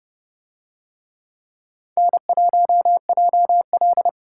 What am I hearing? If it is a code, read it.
Morse code, heard as a keyed tone, about 27 wpm, 700 Hz. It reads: N1JL